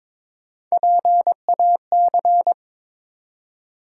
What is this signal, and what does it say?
Morse code, heard as a keyed tone, about 22 wpm, 700 Hz. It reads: PAC
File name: ARRL Sections v2 - 22wpm